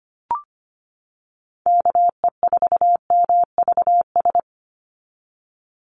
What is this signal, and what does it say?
Morse code, heard as a keyed tone, about 25 wpm, 690 Hz. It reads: KE4MVS
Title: USA Call Signs 25wpm